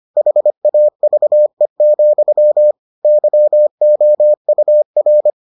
Morse code, heard as a keyed tone, about 25 wpm, 600 Hz. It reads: HAVE, YOUR